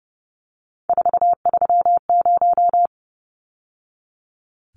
Morse code, heard as a keyed tone, about 30 wpm, 700 Hz. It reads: 430
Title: Numbers - 3-Digit 30wpm